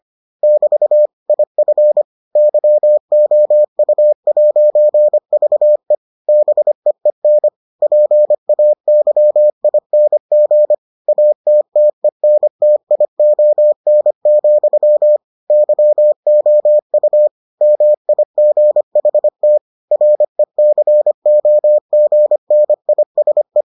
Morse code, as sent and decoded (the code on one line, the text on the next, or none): -...- / .. ..-. / -.-- --- ..- .----. ...- . / -... . -. / .--. .- -.-- .. -. --. / .- - - . -. - .. --- -. --..-- / -.-- --- ..- / -- .. --. .... - / .-. -.-. --- --. -. .. ... .
<BT> IF YOU'VE BEEN PAYING ATTENTION, YOU MIGHT RECOGNISE